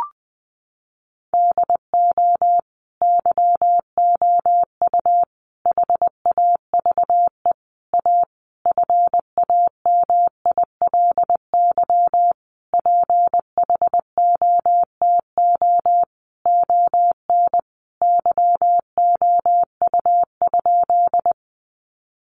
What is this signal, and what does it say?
Morse code, heard as a keyed tone, about 20 wpm, 700 Hz. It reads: DO YOU HAVE A FAMILY PHOTO ON YOU?
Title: Sentences from Top 1200 Words 20wpm